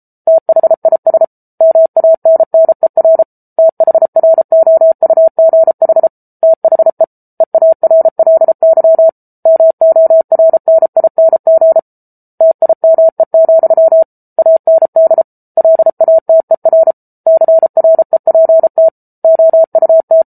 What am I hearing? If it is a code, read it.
Morse code, heard as a keyed tone, about 34 wpm, 650 Hz. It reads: THIS MANNER THROUGH THE EARLY MORNING TIME, AND LATER CREPT OUT